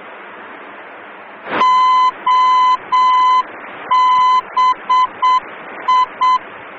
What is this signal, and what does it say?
Morse code, heard as a keyed tone, about 7 wpm, 1010 Hz. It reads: OBI
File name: germany / mx / OBI-D